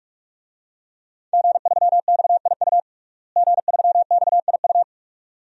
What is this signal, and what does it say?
Morse code, heard as a keyed tone, about 46 wpm, 700 Hz. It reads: G3XIU G3XIU
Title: DXCC-100 Call Signs - 5 Characters - Repeated 2x - 45wpm